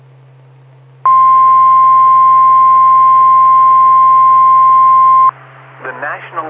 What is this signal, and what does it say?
Emergency Alert System (SAME) tones (the 1050 Hz attention tone), with speech. It is heard from 1.0 to 5.4 s.